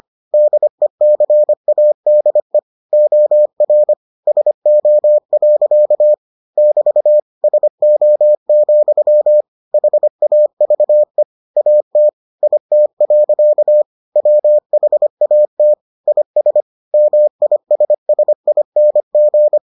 Morse code, heard as a keyed tone, about 25 wpm, 600 Hz. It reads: DECADE OR SO. <BT> SO, HAVE AT IT. WHAT IS MISSING